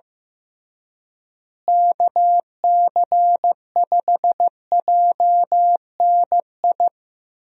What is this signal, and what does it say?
Morse code, heard as a keyed tone, about 15 wpm, 700 Hz. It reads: KC5JNI
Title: USA Call Signs - 6 Characters - Repeated 3x - 15wpm